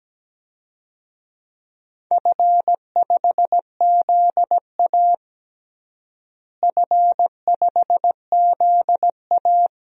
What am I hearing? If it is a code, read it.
Morse code, heard as a keyed tone, about 17 wpm, 700 Hz. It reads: F5ZA F5ZA
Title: DXCC-100 Call Signs - 4 Characters - Repeated 2x - 17wpm